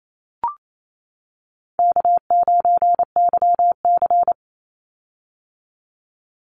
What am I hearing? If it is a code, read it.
Morse code, heard as a keyed tone, about 28 wpm, 700 Hz. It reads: K9YC
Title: Easy Call Signs - 28wpm